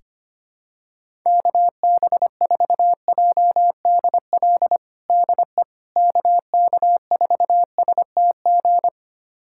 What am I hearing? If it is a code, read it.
Morse code, heard as a keyed tone, about 25 wpm, 700 Hz. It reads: KB4JDL DE KK4STG
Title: USA Callsign DE Callsign 25wpm